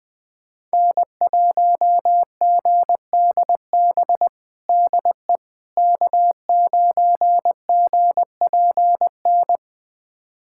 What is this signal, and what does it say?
Morse code, heard as a keyed tone, about 20 wpm, 700 Hz. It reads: N1GDB DE K9GPN
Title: USA Callsign DE Callsign - 5 Characters - 20wpm